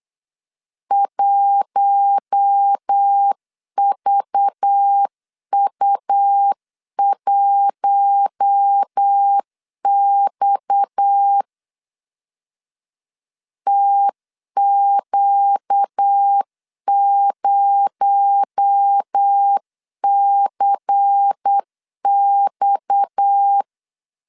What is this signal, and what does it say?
Morse code, heard as a keyed tone, about 8 wpm, 800 Hz. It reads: 1VU1X TQ0CX
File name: CW / DET / 1 6 1500C S C 8wpm 5219